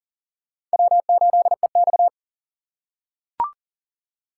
Morse code, heard as a keyed tone, about 37 wpm, 700 Hz. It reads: W8EX
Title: USA Call Signs - 4 Characters - 4x Character Spacing - 40wpm